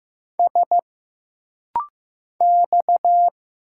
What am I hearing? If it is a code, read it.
Morse code, heard as a keyed tone, about 15 wpm, 700 Hz. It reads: S X
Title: Single Letters - Rapid-Fire 15wpm